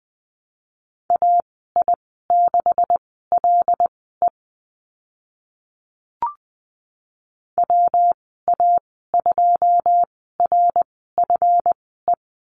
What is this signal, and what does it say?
Morse code, heard as a keyed tone, about 20 wpm, spaced as for 14 wpm, 700 Hz. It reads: AI6LE WA2RFE
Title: 140 Lesson 23g - Letter f - Call Signs 20wpm